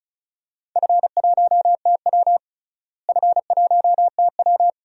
Morse code, heard as a keyed tone, about 34 wpm, 700 Hz. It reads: F1TW F1TW
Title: DXCC-100 Call Signs - 4 Characters - Repeated 3x - 35wpm